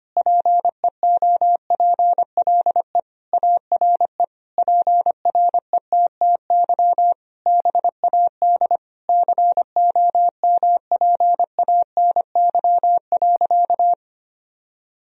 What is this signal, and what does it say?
Morse code, heard as a keyed tone, about 25 wpm, 700 Hz. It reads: PEOPLE ARE PRETTY BAD COMPANY.
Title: Sentences from Top 1000 Words - Review Part 3 - 25wpm